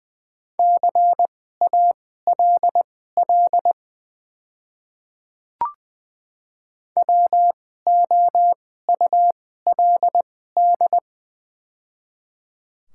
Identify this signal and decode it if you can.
Morse code, heard as a keyed tone, about 20 wpm, spaced as for 15 wpm, 700 Hz. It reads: CALL WOULD
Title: 2x Character Spacing - Top 200 Words - 20wpm